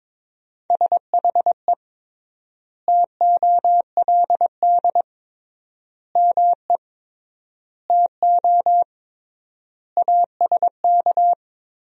Morse code, heard as a keyed tone, about 22 wpm, 700 Hz. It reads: SHE TOLD ME TO ASK
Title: 3x Word-Spacing - Sentences from Top 400 Words - 22wpm